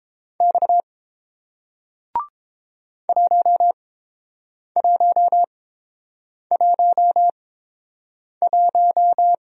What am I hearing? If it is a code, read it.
Morse code, heard as a keyed tone, about 30 wpm, spaced as for 7 wpm, 700 Hz. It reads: X 1111